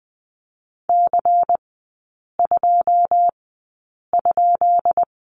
Morse code, heard as a keyed tone, about 20 wpm, spaced as for 8 wpm, 690 Hz. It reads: C2?
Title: 133 Lesson 22e - Question Mark - Three Characters 20wpm